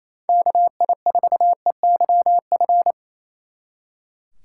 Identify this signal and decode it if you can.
Morse code, heard as a keyed tone, about 33 wpm, 700 Hz. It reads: KI4EYF
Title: Speed Racing - USA 2x3 Call Signs 28wpm